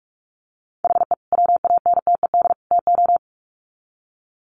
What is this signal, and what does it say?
Morse code, heard as a keyed tone, about 44 wpm, 700 Hz. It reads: HE WANTED TO
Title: Sets of 3 Words 45wpm